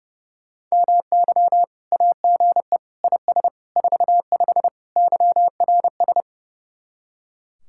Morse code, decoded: MY AGE IS 45 YRS

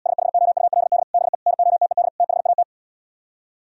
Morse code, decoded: SHOULD BE NOTED THAT

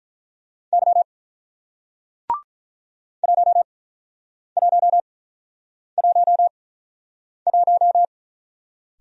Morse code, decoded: Y 1111

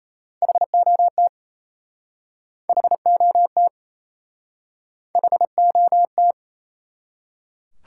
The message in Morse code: HOT HOT HOT

